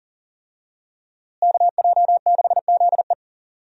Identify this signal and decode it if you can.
Morse code, heard as a keyed tone, about 40 wpm, 700 Hz. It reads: KJ6ZE
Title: USA Call Signs - 5 Characters - Repeated 3x - 40wpm